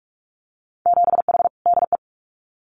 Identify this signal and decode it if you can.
Morse code, heard as a keyed tone, about 46 wpm, 700 Hz. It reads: 7H DE
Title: Winter Field Day Exchanges Encore 45wpm